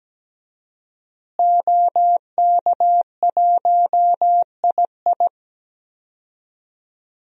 Morse code, heard as a keyed tone, about 17 wpm, 700 Hz. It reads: OK1II
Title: DXCC-100 Call Signs - 5 Characters - Repeated 3x - 17wpm